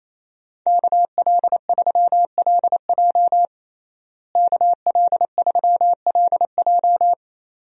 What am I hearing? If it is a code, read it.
Morse code, heard as a keyed tone, about 28 wpm, 700 Hz. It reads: KL3LJ KL3LJ
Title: USA Call Signs - 5 Characters - Repeated 2x - 28wpm